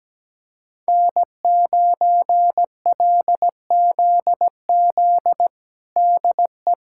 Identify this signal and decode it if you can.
Morse code, heard as a keyed tone, about 17 wpm, 700 Hz. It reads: N9LZZ DE
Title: USA Callsign DE Callsign - 5 Characters - 17wpm